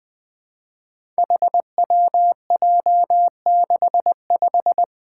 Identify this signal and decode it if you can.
Morse code, heard as a keyed tone, about 20 wpm, 700 Hz. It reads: HWJ65
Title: Five Challenging Letter-Number Combinations 20wpm